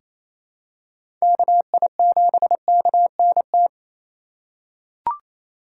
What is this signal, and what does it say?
Morse code, heard as a keyed tone, about 28 wpm, 690 Hz. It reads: KI7KNT
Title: USA Call Signs - 6 Characters - 2x Character Spacing - 28wpm